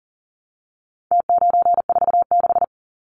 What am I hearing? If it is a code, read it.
Morse code, heard as a keyed tone, about 40 wpm, 700 Hz. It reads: T946
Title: Numbers - Contest Style - 4-Digit 40wpm